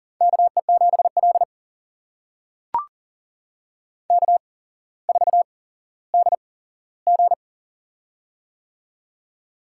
Morse code, heard as a keyed tone, about 38 wpm, spaced as for 17 wpm, 700 Hz. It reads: KE7L K4DG
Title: USA Call Signs - 4 Characters - 8x Character Spacing - 40wpm